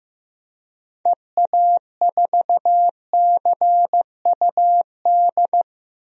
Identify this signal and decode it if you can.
Morse code, heard as a keyed tone, about 15 wpm, 700 Hz. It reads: EA4CUD
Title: DXCC-100 Call Signs - 6 Characters - Repeated 2x - 15wpm